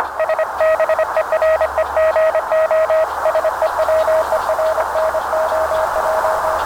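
Morse code, heard as a keyed tone, about 25 wpm, 640 Hz, with radio noise. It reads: SBEREGO SEWERNOJ
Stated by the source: UA4NU worked on home-built QRP gear in the RU-QRP Club "Russkoe Pole" contest, with poor and unstable propagation.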